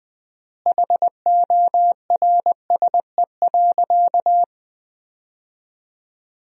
Morse code, heard as a keyed tone, about 20 wpm, 700 Hz. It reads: HORSE.
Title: 4x Word-Spacing - Sentences from Top 300 Words - 20wpm